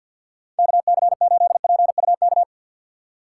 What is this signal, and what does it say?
Morse code, heard as a keyed tone, about 52 wpm, 690 Hz. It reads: KC8PUK
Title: USA Call Signs - 2x Character Spacing - 50wpm